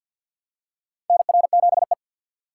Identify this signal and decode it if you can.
Morse code, heard as a keyed tone, about 52 wpm, 700 Hz. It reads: NR7E
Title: USA Call Signs - 4 Characters - 2x Character Spacing - 50wpm